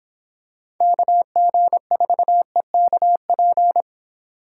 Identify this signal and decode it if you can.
Morse code, heard as a keyed tone, about 25 wpm, 700 Hz. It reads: KG4EKP